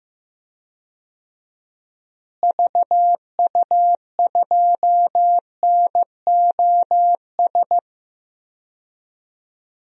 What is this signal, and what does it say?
Morse code, heard as a keyed tone, about 15 wpm, 700 Hz. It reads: VU2NOS
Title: DXCC-100 Call Signs - 6 Characters - Repeated 2x - 15wpm